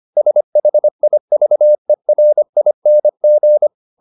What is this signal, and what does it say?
Morse code, heard as a keyed tone, about 25 wpm, 600 Hz. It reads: SHIVERING
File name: the old man and the sea part 1 25wpm